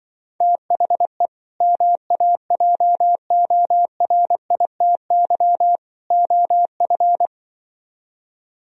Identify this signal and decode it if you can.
Morse code, heard as a keyed tone, about 24 wpm, 700 Hz. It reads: THE MAJORITY OF
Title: Speed-Racing - Sets of 3 Words 20wpm